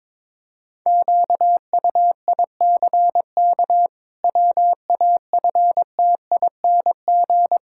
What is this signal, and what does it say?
Morse code, heard as a keyed tone, about 22 wpm, 700 Hz. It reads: QUICK WAFTING
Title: Pangram Challenge v2 - 22wpm